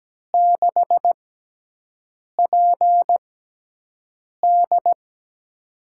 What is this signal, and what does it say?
Morse code, heard as a keyed tone, about 17 wpm, spaced as for 6 wpm, 700 Hz. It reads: 6PD
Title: USA Call Signs - 5 Characters - 6x Character Spacing - 17wpm